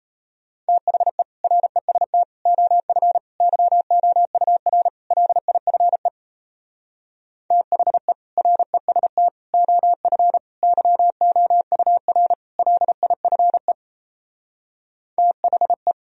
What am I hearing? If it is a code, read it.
Morse code, heard as a keyed tone, about 36 wpm, 700 Hz. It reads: THE REST OF YOUR LIFE THE REST OF YOUR LIFE THE